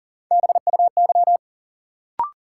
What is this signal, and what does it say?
Morse code, heard as a keyed tone, about 40 wpm, 700 Hz. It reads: BUY
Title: Sentences from Top 700 Words - New Word Drill - 40wpm